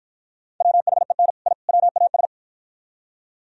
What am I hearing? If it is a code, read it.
Morse code, heard as a keyed tone, about 47 wpm, 700 Hz. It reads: WHEN I WAS